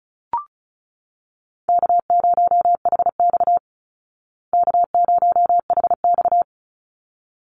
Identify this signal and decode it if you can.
Morse code, heard as a keyed tone, about 34 wpm, 700 Hz. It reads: K0HX K0HX
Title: Easy Call Signs Repeated 2x - 35wpm